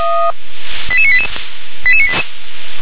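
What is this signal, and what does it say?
Five-tone selective call (ZVEI), heard as quick runs of five tones, tones of about 70 ms. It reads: 88988
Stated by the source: ZVEI five-tone call received by SDR.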